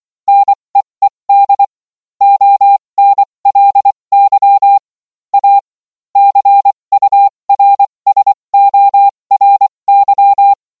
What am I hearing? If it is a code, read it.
Morse code, heard as a keyed tone, about 24 wpm, spaced as for 20 wpm, 800 Hz. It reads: NEED ONLY A CURSORY